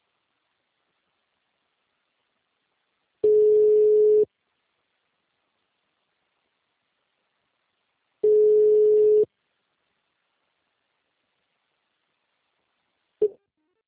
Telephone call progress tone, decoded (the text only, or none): ringback tone